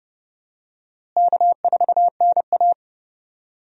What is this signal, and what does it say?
Morse code, heard as a keyed tone, about 30 wpm, 700 Hz. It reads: K4NA
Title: USA Call Signs - 4 Characters - Repeated 2x - 30wpm